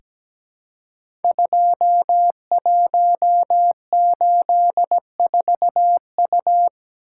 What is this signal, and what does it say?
Morse code, heard as a keyed tone, about 17 wpm, 700 Hz. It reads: 2184U
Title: Code Groups - 5 Letter:Numbers - 17wpm